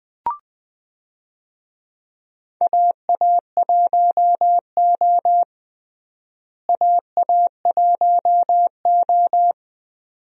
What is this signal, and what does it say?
Morse code, heard as a keyed tone, about 20 wpm, 700 Hz. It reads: AA1O AA1O
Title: USA Call Signs - 4 Characters - Repeated 2x - 20wpm